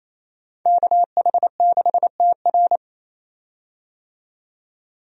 Morse code, heard as a keyed tone, about 28 wpm, 700 Hz. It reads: KH6TR
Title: DXCC-100 Call Signs - 5 Characters - 28wpm